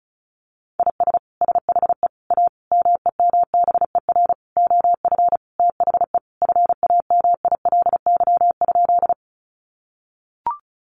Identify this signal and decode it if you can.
Morse code, heard as a keyed tone, about 34 wpm, 700 Hz. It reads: IS SHE A MEMBER OF THE FAMILY?
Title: Sentences from Top 1100 Words 35wpm